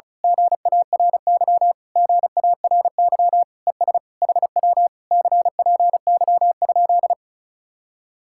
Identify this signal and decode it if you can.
Morse code, heard as a keyed tone, about 32 wpm, 700 Hz. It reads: GARY GARY ES HW CPY?
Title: Speed-Racing Common QSO Phrases Encore - 35wpm